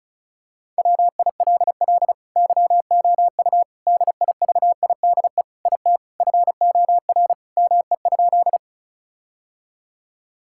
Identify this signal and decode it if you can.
Morse code, heard as a keyed tone, about 36 wpm, 700 Hz. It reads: WILL YOU DIVIDE IT FOR ME?
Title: Sentences from Top 1600 Words 35wpm